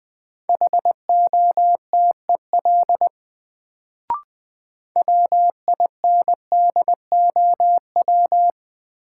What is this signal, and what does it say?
Morse code, heard as a keyed tone, about 21 wpm, 700 Hz. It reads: HOTEL WINDOW
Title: Sentences from Top 1600 Words - New Word Drill - 20wpm